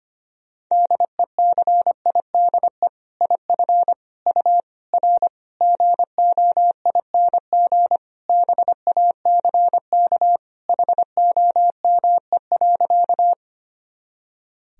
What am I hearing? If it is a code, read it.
Morse code, heard as a keyed tone, about 25 wpm, 700 Hz. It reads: DECIDE IF U R GOING BACK HOME.
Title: Sentences from Top 500 Words and Abbreviations 25wpm